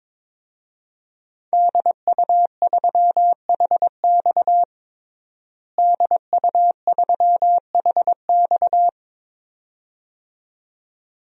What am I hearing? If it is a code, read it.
Morse code, heard as a keyed tone, about 22 wpm, 700 Hz. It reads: DU3HX DU3HX
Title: DXCC-100 Call Signs - 5 Characters - Repeated 2x - 22wpm